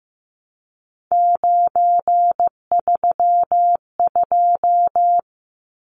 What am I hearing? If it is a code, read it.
Morse code, heard as a keyed tone, about 15 wpm, 700 Hz. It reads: 932
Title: Numbers - 3-Digit 15wpm